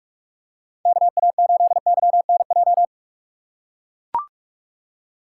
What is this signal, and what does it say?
Morse code, heard as a keyed tone, about 43 wpm, 690 Hz. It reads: KA8YNJ
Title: USA Call Signs - 4x Character Spacing - 45wpm